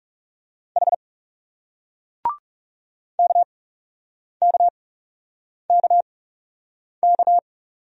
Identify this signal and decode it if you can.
Morse code, heard as a keyed tone, about 40 wpm, spaced as for 8 wpm, 700 Hz. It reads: H KKKK